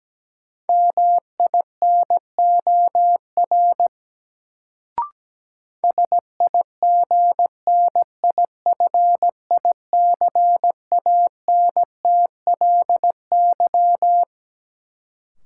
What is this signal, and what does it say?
Morse code, heard as a keyed tone, about 17 wpm, 700 Hz. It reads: MINOR SIGNIFICANTLY
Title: Sentences from Top 1500 Words - New Word Drill - 17wpm